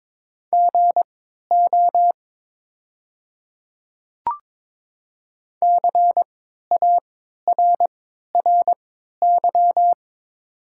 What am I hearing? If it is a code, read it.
Morse code, heard as a keyed tone, about 23 wpm, spaced as for 13 wpm, 690 Hz. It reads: GO CARRY
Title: 3x Character Spacing - Top 500 Words - 22wpm